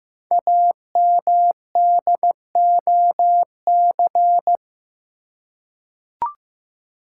Morse code, heard as a keyed tone, about 15 wpm, 700 Hz. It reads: AMDOC
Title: Code Groups - 5 Letters - 15wpm